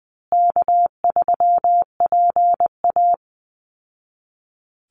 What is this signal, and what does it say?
Morse code, heard as a keyed tone, about 20 wpm, 700 Hz. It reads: K3PA
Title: Easy Call Signs Repeated 2x - 20wpm